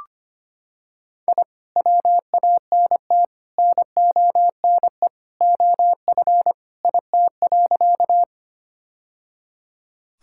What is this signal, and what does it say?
Morse code, heard as a keyed tone, about 25 wpm, 700 Hz. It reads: I WANT NONE OF IT.
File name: practice / 25wpm / Sentences with 2-Word Phrases - 25wpm